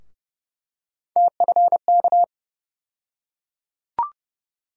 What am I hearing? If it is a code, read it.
Morse code, heard as a keyed tone, about 31 wpm, 700 Hz. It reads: TFK
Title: USA Call Sign Postfixes - 3 Letters 30wpm